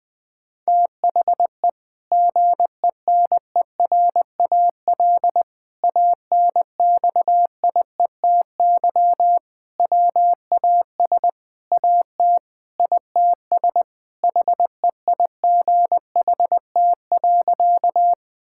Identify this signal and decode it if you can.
Morse code, heard as a keyed tone, about 20 wpm, 700 Hz. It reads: THE GENERAL ANXIETY WAS AT ITS HEIGHT.
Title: Sentences from Top 1600 Words 20wpm